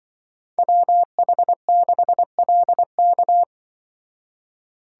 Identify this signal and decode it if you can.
Morse code, heard as a keyed tone, about 24 wpm, 700 Hz. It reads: WH6LK
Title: Speed Racing - USA 2x2 Call Signs 20wpm